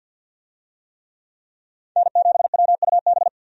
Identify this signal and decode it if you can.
Morse code, heard as a keyed tone, about 53 wpm, 700 Hz. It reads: N7WUB
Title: USA Call Signs - 5 Characters - Repeated 2x - 50wpm